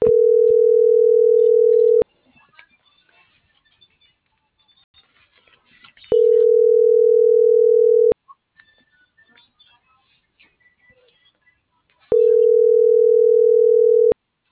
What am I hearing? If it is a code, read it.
Ringback tone (North America), 440 + 480 Hz.